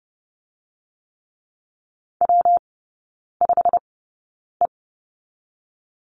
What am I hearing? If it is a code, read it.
Morse code, heard as a keyed tone, about 30 wpm, spaced as for 9 wpm, 700 Hz. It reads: W5E
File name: Lesson 19e - Letter w - Three Characters 30wpm